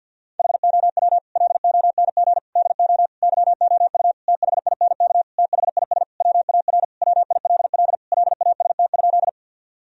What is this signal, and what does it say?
Morse code, heard as a keyed tone, about 50 wpm, 700 Hz. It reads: HOW LONG DO YOU THINK THIS WAR WILL LAST?
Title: Sentences from Top 1000 Words - Review Part 2 - 50wpm